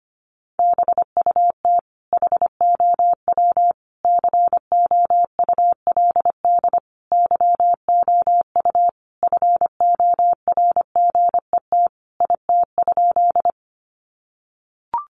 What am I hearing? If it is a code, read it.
Morse code, heard as a keyed tone, about 25 wpm, 700 Hz. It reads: BUT HOW COULD YOU FORGET IT?
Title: Sentences from Top 1200 Words 25wpm